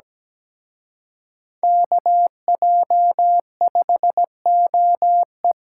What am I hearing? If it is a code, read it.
Morse code, heard as a keyed tone, about 17 wpm, 700 Hz. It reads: KJ5OE
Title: Speed-Racing Common QSO Phrases - 17wpm